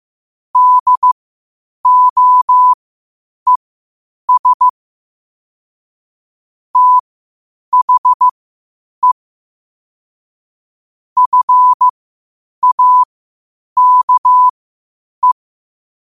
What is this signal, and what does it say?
Morse code, heard as a keyed tone, about 15 wpm, spaced as for 8 wpm, 1000 Hz. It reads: DOES THE FAKE
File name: qotdfiles / 8wpm